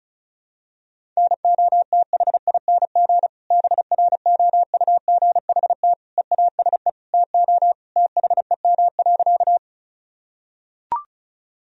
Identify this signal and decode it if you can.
Morse code, heard as a keyed tone, about 34 wpm, 700 Hz. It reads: NOTHING BROUGHT EASE TO THEM.
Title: Sentences from Top 1000 Words - Review Part 1 - 35wpm